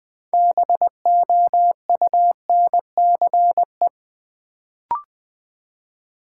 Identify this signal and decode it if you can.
Morse code, heard as a keyed tone, about 20 wpm, 690 Hz. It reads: BOUNCE
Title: Build A Word - 2 Letters At A Time - 6-Letter Words - 20wpm